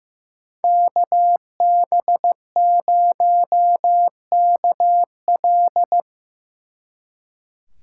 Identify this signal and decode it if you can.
Morse code, heard as a keyed tone, about 17 wpm, 700 Hz. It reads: KB0KL
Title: Speed Racing - USA Call Signs 15wpm